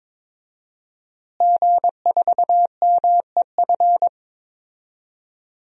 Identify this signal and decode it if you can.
Morse code, heard as a keyed tone, about 22 wpm, 700 Hz. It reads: G4MEF
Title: DXCC-100 Call Signs - 5 Characters - Repeated 2x - 22wpm